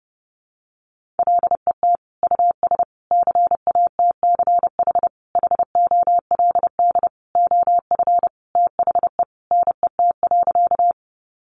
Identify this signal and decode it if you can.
Morse code, heard as a keyed tone, about 30 wpm, 700 Hz. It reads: LET US CATCH HOLD OF THE NET.
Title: Sentences from Top 1500 Words 30wpm